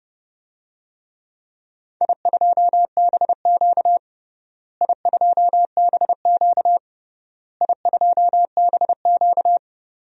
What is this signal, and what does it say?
Morse code, heard as a keyed tone, about 31 wpm, 700 Hz. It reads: I2BQ I2BQ I2BQ
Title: DXCC-100 Call Signs - 4 Characters - Repeated 3x - 30wpm